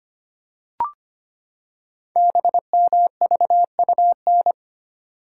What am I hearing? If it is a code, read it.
Morse code, heard as a keyed tone, about 25 wpm, 700 Hz. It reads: BMVUN
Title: Code Groups - 5 Letters - 25wpm